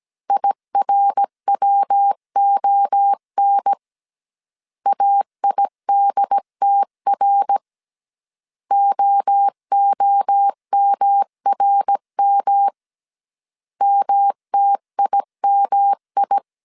Morse code, heard as a keyed tone, about 17 wpm, 800 Hz. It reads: IRWON AIDTR OOMRM MTIMI
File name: CW / DET / 1 3 1500C S A 16wpm 2141